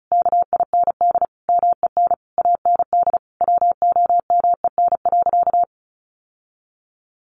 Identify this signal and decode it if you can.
Morse code, heard as a keyed tone, about 34 wpm, 700 Hz. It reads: KIND MEN AND WOMEN.
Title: Sentences from Top 1300 Words 35wpm